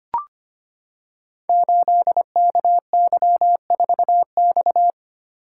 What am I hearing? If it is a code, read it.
Morse code, heard as a keyed tone, about 25 wpm, 700 Hz. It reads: 8KY4X
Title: Code Groups - 5 Letter:Numbers - 25wpm